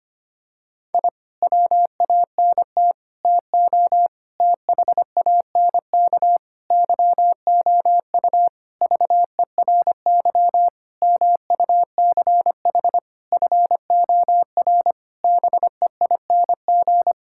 Morse code, heard as a keyed tone, about 25 wpm, 700 Hz. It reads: I WANT TO THANK YOU VERY MUCH FOR BEING
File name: practice / 25wpm / Sentences with 9-Word Phrases - 25wpm